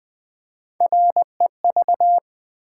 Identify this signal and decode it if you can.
Morse code, heard as a keyed tone, about 20 wpm, 690 Hz. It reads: REV
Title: Build A Word - 3 Letters At A Time - 6-Letter Words - 20wpm